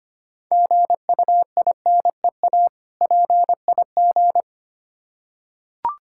Morse code, heard as a keyed tone, about 25 wpm, 700 Hz. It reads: GUINEA PIG
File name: practice / 25wpm / Difficult Words with Hints - 25wpm